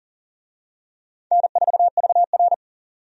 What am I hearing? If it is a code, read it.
Morse code, heard as a keyed tone, about 40 wpm, 700 Hz. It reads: N4VR
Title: USA Call Signs - 4 Characters - 40wpm